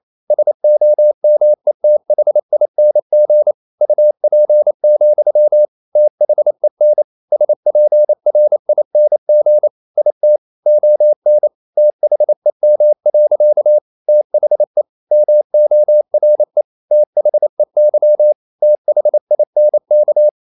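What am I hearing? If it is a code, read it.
Morse code, heard as a keyed tone, about 28 wpm, 600 Hz. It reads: SOMETHING UP, THEN SPRING IT ON THEM. THE MORE THEY THINK
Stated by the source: Morse code practice at 28 wpm, text of Ken Follett's novel Never in English, from CW med Gnister.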